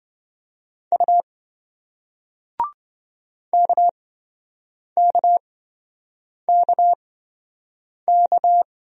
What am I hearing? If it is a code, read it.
Morse code, heard as a keyed tone, about 28 wpm, spaced as for 7 wpm, 700 Hz. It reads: U KKKK